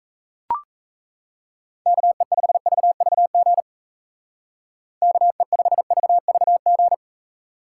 Morse code, heard as a keyed tone, about 33 wpm, 700 Hz. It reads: KE5VVG KE5VVG
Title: Speed Racing - USA 2x3 Call Signs 28wpm